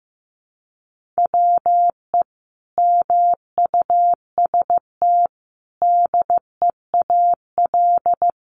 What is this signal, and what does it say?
Morse code, heard as a keyed tone, about 15 wpm, 700 Hz. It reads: WE MUST DEAL
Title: Sentences from Top 1400 Words 15wpm